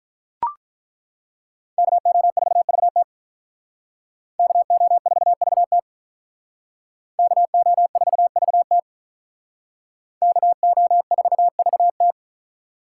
Morse code, heard as a keyed tone, about 42 wpm, 700 Hz. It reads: KO4VT KO4VT KO4VT KO4VT
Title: Speed Racing - USA 2x2 Call Signs 35wpm